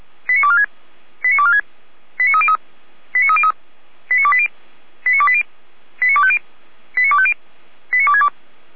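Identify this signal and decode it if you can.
Five-tone selective call (ZVEI), heard as quick runs of five tones, tones of about 80 ms. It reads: 89246 89246 89303 89303 89280 89280 89250 89250 89262
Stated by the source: ZVEI five-tone selective call received on SDR, logged by the Feuerwehr Orlamünde fire brigade server.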